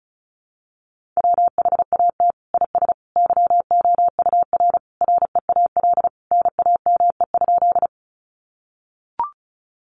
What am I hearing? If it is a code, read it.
Morse code, heard as a keyed tone, about 34 wpm, 700 Hz. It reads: WHAT IS YOUR REAL NAME?
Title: Sentences from Top 300 Words 35wpm